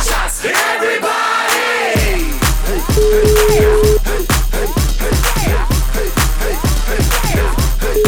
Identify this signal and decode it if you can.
Ringback tone (Europe, 425 Hz), with noise.